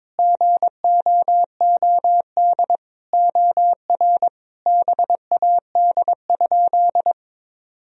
Morse code, heard as a keyed tone, about 22 wpm, 700 Hz. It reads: GOOD OR BAD?
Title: Sentences from Top 1000 Words - Review Part 1 - 22wpm